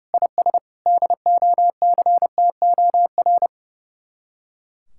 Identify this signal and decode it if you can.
Morse code, heard as a keyed tone, about 40 wpm, spaced as for 35 wpm, 700 Hz. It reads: IS DOCTOR
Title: Speed-Racing Common QSO Phrases - 30wpm